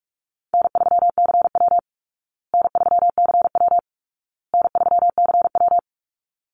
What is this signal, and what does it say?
Morse code, heard as a keyed tone, about 44 wpm, 700 Hz. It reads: N3CW N3CW N3CW
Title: Easy Call Signs Repeated 3x - 45wpm